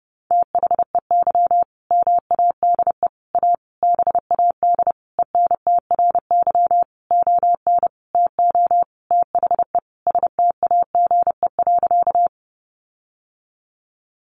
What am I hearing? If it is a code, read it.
Morse code, heard as a keyed tone, about 30 wpm, 700 Hz. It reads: THEY MADE A BAD ENTRY ON TO THE STAGE.
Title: Sentences from Top 1500 Words 30wpm